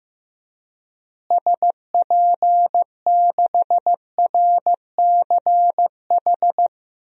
Morse code, heard as a keyed tone, about 15 wpm, 700 Hz. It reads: SP6RCH
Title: DXCC-100 Call Signs - 6 Characters - Repeated 3x - 15wpm